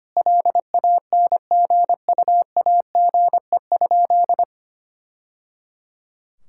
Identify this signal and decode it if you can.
Morse code, heard as a keyed tone, about 25 wpm, 700 Hz. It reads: LANGUAGE?
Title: Sentences from Top 1000 Words - Review Part 5 - 25wpm